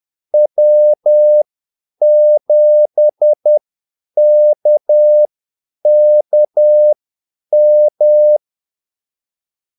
Morse code, heard as a keyed tone, about 10 wpm, spaced as for 8 wpm, 600 Hz. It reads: W7KKM
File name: Callsigns-10-8wpm